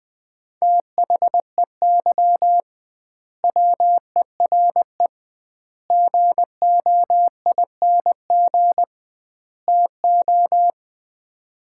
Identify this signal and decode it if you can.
Morse code, heard as a keyed tone, about 20 wpm, 700 Hz. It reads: THEY WERE GOING TO
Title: 2x Word-Spacing - Sets of 4 Words - 20wpm